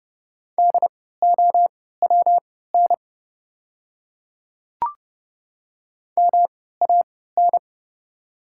Morse code, heard as a keyed tone, about 30 wpm, spaced as for 17 wpm, 700 Hz. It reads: DOWN MAN
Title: 3x Character Spacing - Top 300 Words - 30wpm